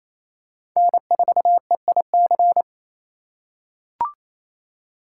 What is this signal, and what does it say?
Morse code, heard as a keyed tone, about 28 wpm, 700 Hz. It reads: N4EIC
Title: USA Call Signs - 5 Characters - 6x Character Spacing - 28wpm